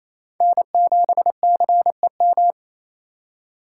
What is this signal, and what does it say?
Morse code, heard as a keyed tone, about 28 wpm, 700 Hz. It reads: N7CEM
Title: USA Call Signs - 2x Character Spacing - 28wpm